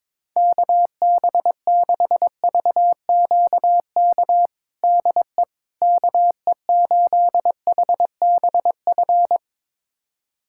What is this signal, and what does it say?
Morse code, heard as a keyed tone, about 22 wpm, 700 Hz. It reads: KB6VQK DE KE8HBF
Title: USA Callsign DE Callsign - 6 Characters - 22wpm